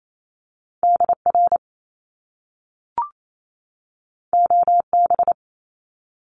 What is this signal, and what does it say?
Morse code, heard as a keyed tone, about 28 wpm, 700 Hz. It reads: DR OB
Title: Top Two Letter Combinations in Words 28wpm